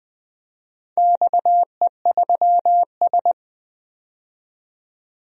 Morse code, heard as a keyed tone, about 21 wpm, 700 Hz. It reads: XE3S
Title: DXCC-100 Call Signs - 4 Characters - 20wpm